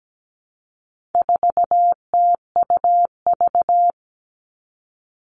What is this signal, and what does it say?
Morse code, heard as a keyed tone, about 17 wpm, 700 Hz. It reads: 4TUV
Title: All Cut Numbers - 4-Digit 17wpm